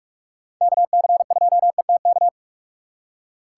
Morse code, heard as a keyed tone, about 43 wpm, 700 Hz. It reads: KC2ETK